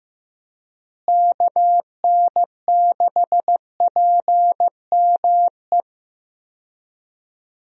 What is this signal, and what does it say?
Morse code, heard as a keyed tone, about 16 wpm, 700 Hz. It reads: KN6PME
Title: USA Call Signs - 6 Characters - Repeated 2x - 15wpm